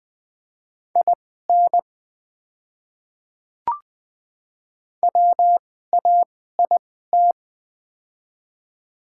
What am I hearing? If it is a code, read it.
Morse code, heard as a keyed tone, about 20 wpm, spaced as for 14 wpm, 700 Hz. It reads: IN WAIT
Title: 2x Character Spacing - Top 500 Words - 20wpm